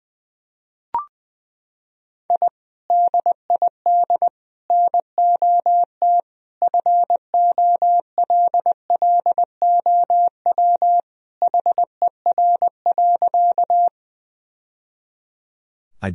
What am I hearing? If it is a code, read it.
Morse code, heard as a keyed tone, about 20 wpm, 700 Hz. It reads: I DID NOT FOLLOW HER.
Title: Sentences from Top 1000 Words - Review Part 1 - 20wpm